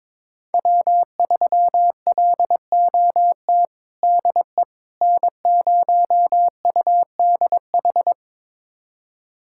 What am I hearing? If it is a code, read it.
Morse code, heard as a keyed tone, about 22 wpm, 700 Hz. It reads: W3LOT DE N0UDH
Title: USA Callsign DE Callsign - 5 Characters - 22wpm